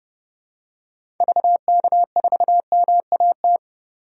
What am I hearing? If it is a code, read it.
Morse code, heard as a keyed tone, about 32 wpm, 700 Hz. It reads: VK4MAT
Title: DXCC-100 Call Signs - 6 Characters - 30wpm